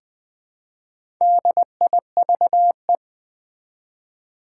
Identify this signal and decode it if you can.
Morse code, heard as a keyed tone, about 20 wpm, 700 Hz. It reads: DIVE
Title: Build A Word - 1 Letter At A Time - 4-Letter Words - 20wpm